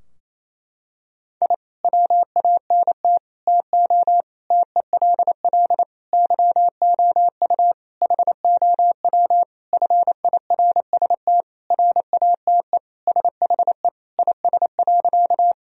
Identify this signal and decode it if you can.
Morse code, heard as a keyed tone, about 28 wpm, 700 Hz. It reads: I WANT TO TELL YOU HOW FIRST RATE SHE IS.
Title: Sentences from Top 1000 Words - Review Part 4 - 28wpm